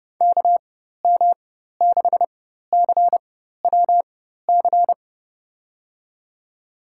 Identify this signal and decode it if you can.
Morse code, heard as a keyed tone, about 30 wpm, spaced as for 14 wpm, 700 Hz. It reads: KM6CWC
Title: USA Call Signs - 6 Characters - 4x Character Spacing - 30wpm